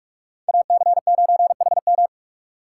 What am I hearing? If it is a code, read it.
Morse code, heard as a keyed tone, about 44 wpm, 700 Hz. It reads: AC9HM
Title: USA Call Signs - 5 Characters - 6x Character Spacing - 45wpm